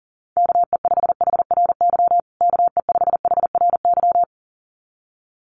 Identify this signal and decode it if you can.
Morse code, heard as a keyed tone, about 40 wpm, 700 Hz. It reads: KE5HRY KE5HRY
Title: USA Call Signs Repeated 2x 40wpm